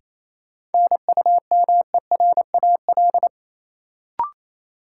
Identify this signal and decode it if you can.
Morse code, heard as a keyed tone, about 28 wpm, 700 Hz. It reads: NUMERAL
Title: Sentences from Top 1000 Words - Review Part 2 - New Word Drill - 28wpm